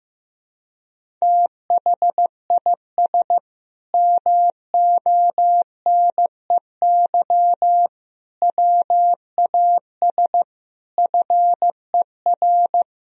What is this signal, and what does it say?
Morse code, heard as a keyed tone, about 15 wpm, 700 Hz. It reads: THIS MONEY WAS FER
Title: Sentences from Top 400 Words and Abbreviations 15wpm